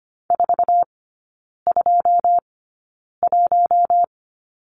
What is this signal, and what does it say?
Morse code, heard as a keyed tone, about 25 wpm, spaced as for 9 wpm, 700 Hz. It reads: 421
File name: course / 25:10wpm / 093 Lesson 16e - Letter c - Three Characters 25wpm